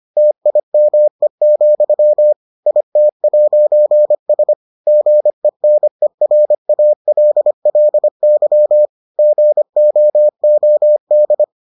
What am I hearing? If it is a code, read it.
Morse code, heard as a keyed tone, about 25 wpm, 600 Hz. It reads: TIME, IT'S GENERALLY GOOD